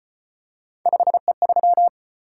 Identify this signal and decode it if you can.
Morse code, heard as a keyed tone, about 33 wpm, 700 Hz. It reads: 5E3